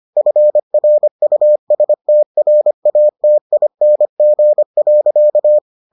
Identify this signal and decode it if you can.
Morse code, heard as a keyed tone, about 25 wpm, 600 Hz. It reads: FRUSTRATING.